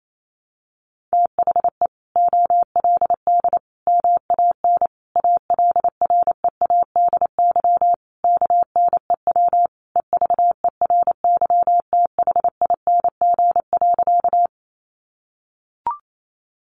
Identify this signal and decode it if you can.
Morse code, heard as a keyed tone, about 28 wpm, 700 Hz. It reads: THE OLD MAN ALREADY KNEW EVERYTHING.
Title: Sentences from Top 700 Words 28wpm